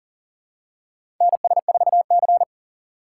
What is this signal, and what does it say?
Morse code, heard as a keyed tone, about 40 wpm, 700 Hz. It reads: NS4C
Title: USA Call Signs - 3x Character Spacing - 40wpm